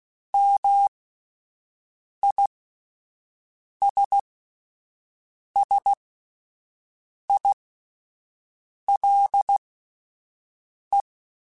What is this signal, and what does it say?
Morse code, heard as a keyed tone, about 16 wpm, spaced as for 5 wpm, 800 Hz. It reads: MISSILE
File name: Misc / Misc-mp3s / Words @5